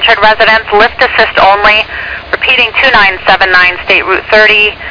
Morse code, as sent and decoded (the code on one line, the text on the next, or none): none